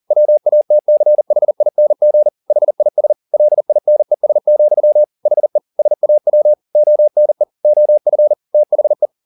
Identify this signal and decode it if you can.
Morse code, heard as a keyed tone, about 40 wpm, 600 Hz. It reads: WATCHING HIS LINES, HE SAW ONE OF THE